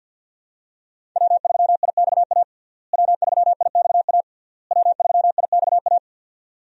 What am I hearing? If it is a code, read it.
Morse code, heard as a keyed tone, about 48 wpm, 700 Hz. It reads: W3IXA W3IXA W3IXA
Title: USA Call Signs - 5 Characters - Repeated 3x - 50wpm